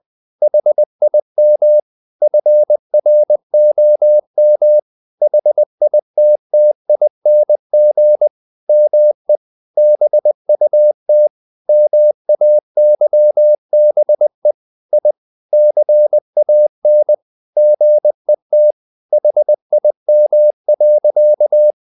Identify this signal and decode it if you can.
Morse code, heard as a keyed tone, about 20 wpm, 600 Hz. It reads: HIM FROM HITTING ME BUT MAYBE I CAN GET HIM.
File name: the old man and the sea part 2 20wpm